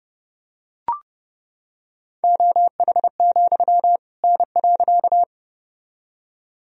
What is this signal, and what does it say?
Morse code, heard as a keyed tone, about 30 wpm, 700 Hz. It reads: OH, N.
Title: Sentences from Top 500 Words and Abbreviations 30wpm